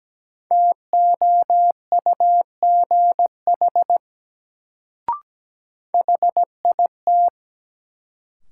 Morse code, heard as a keyed tone, about 17 wpm, 700 Hz. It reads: TOUGH HIT
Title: Sentences from Top 1200 Words - New Word Drill Alt - 17wpm